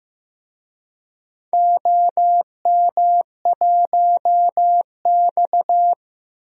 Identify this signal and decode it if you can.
Morse code, heard as a keyed tone, about 16 wpm, 700 Hz. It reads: OM1X
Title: DXCC-100 Call Signs - 4 Characters - 15wpm